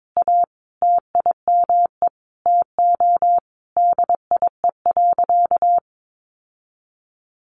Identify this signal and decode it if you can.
Morse code, heard as a keyed tone, about 22 wpm, 700 Hz. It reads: A TIME TO DIE.
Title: Sentences from Top 1300 Words 22wpm